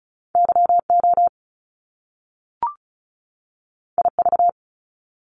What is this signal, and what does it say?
Morse code, heard as a keyed tone, about 36 wpm, 700 Hz. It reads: YO IV
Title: Top Two Letter Combinations in Words 35wpm